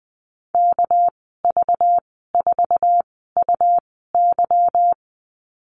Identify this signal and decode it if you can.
Morse code, heard as a keyed tone, about 19 wpm, spaced as for 14 wpm, 700 Hz. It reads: KV4UY
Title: 238 Lesson 38g - Letter q - Call Signs 20wpm